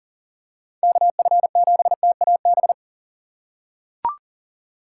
Morse code, heard as a keyed tone, about 40 wpm, 700 Hz. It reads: KF7TAB
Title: USA Call Signs - 6 Characters - 6x Character Spacing - 40wpm